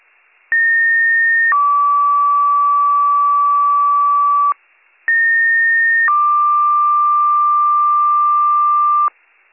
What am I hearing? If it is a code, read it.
Two-tone page, heard as a one-second tone, then a three-second tone, about 1850 then 1150 Hz, about 1850 then 1180 Hz. It is heard from 0.5 to 9.1 s.